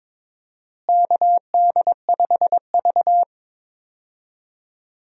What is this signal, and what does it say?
Morse code, heard as a keyed tone, about 22 wpm, 700 Hz. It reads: KD5V
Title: USA Call Signs - 4 Characters - Repeated 3x - 22wpm